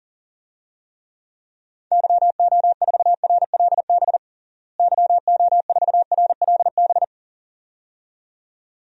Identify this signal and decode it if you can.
Morse code, heard as a keyed tone, about 40 wpm, 700 Hz. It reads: YO4RLB YO4RLB
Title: DXCC-100 Call Signs - 6 Characters - Repeated 2x - 40wpm